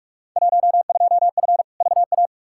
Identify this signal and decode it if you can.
Morse code, heard as a keyed tone, about 44 wpm, 700 Hz. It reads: 12F VA
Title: Field Day Exchanges Encore - 45wpm